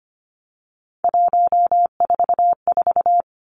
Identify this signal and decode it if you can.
Morse code, heard as a keyed tone, about 25 wpm, 700 Hz. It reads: 144
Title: Three Challenging Letter-Number Combinations 25wpm